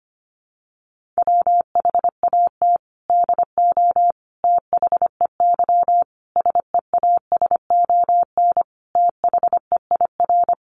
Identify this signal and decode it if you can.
Morse code, heard as a keyed tone, about 25 wpm, 700 Hz. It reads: WHAT DO THEY SEASON THEIR
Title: Sentences from Top 1400 Words 25wpm